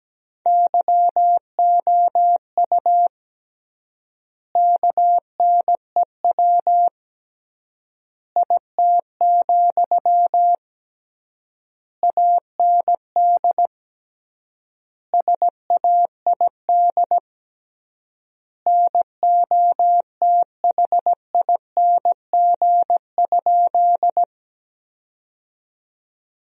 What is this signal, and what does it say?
Morse code, heard as a keyed tone, about 17 wpm, 700 Hz. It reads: YOU KNEW IT, AND SAID NOTHING?
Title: 3x Word-Spacing - Sentences from Top 500 Words - 17wpm